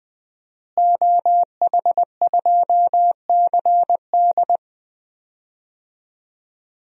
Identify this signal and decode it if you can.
Morse code, heard as a keyed tone, about 20 wpm, 700 Hz. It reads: OH2CD
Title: DXCC-100 Call Signs - 5 Characters - Repeated 2x - 20wpm